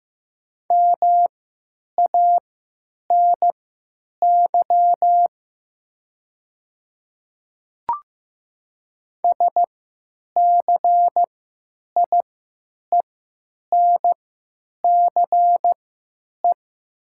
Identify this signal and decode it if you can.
Morse code, heard as a keyed tone, about 15 wpm, spaced as for 9 wpm, 700 Hz. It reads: MANY SCIENCE
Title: 3x Character Spacing - Top 500 Words - 15wpm